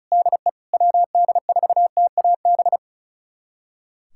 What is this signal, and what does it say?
Morse code, heard as a keyed tone, about 34 wpm, 700 Hz. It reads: DE WD4TAB
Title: USA Callsign DE Callsign - 6 Characters - 35wpm